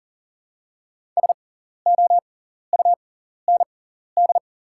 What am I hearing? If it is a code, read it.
Morse code, heard as a keyed tone, about 40 wpm, spaced as for 14 wpm, 700 Hz. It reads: SOUND
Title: 6x Character Spacing - Top 200 Words - 40wpm